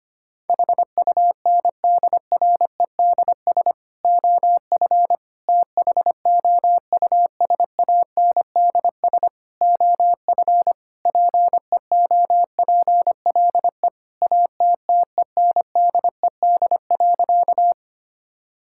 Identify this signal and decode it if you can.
Morse code, heard as a keyed tone, about 25 wpm, 700 Hz. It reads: HUNDREDS OF THOUSANDS OF PEOPLE ATTENDED.
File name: practice / 25wpm / Sentences with 4-Word Phrases - 25wpm